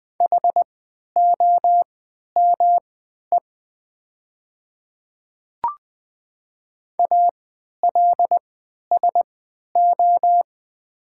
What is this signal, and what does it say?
Morse code, heard as a keyed tone, about 20 wpm, spaced as for 11 wpm, 700 Hz. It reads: HOME ALSO
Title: 3x Character Spacing - Top 200 Words - 20wpm